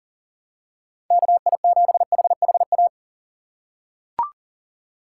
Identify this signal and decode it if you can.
Morse code, heard as a keyed tone, about 40 wpm, 700 Hz. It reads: KI7HHA
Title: USA Call Signs - 6 Characters - 4x Character Spacing - 40wpm